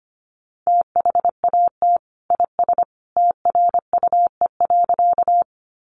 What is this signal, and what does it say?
Morse code, heard as a keyed tone, about 25 wpm, 700 Hz. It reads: THAT IS TRUE.
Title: Sentences from Top 400 Words 25wpm